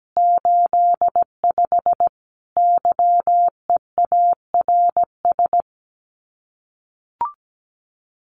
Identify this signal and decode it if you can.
Morse code, heard as a keyed tone, about 17 wpm, 700 Hz. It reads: 85 YEARS
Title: Common QSO Phrases 17wpm